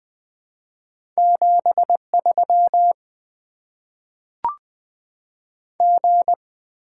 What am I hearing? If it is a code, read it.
Morse code, heard as a keyed tone, about 20 wpm, 700 Hz. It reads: 73 G
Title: Top 20 QSO Elements - 8x Character Spacing - 20wpm